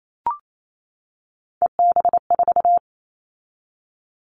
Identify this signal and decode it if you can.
Morse code, heard as a keyed tone, about 28 wpm, 700 Hz. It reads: EB4
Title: Callsign Prefixes for Top 25 DXCC - 28wpm